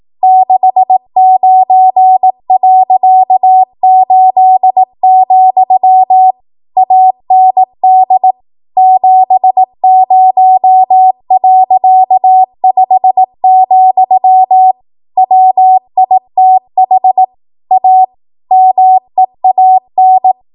Morse code, heard as a keyed tone, about 18 wpm, 750 Hz. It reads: 69.8, AND 70.5, WITH A MEAN
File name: Archive / 18 WPM / 18WPM